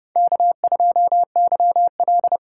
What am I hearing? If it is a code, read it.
Morse code, heard as a keyed tone, about 29 wpm, 700 Hz. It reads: K2YL
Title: USA Call Signs - 3x Character Spacing - 30wpm